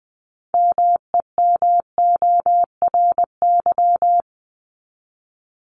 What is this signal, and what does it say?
Morse code, heard as a keyed tone, about 20 wpm, 700 Hz. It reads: MEMORY
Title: Top 1251-1500 Words 20wpm